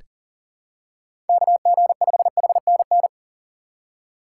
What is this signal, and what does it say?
Morse code, heard as a keyed tone, about 40 wpm, 700 Hz. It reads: KG5HNN
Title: USA Call Signs - 6 Characters - 8x Character Spacing - 40wpm